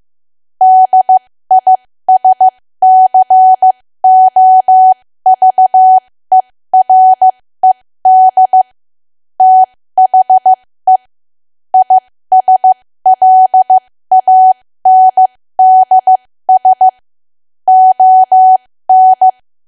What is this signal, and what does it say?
Morse code, heard as a keyed tone, about 15 wpm, spaced as for 13 wpm, 750 Hz. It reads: DISCOVERED THE ISLANDS ON
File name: cwa / W1AW-Level2 / 13WPM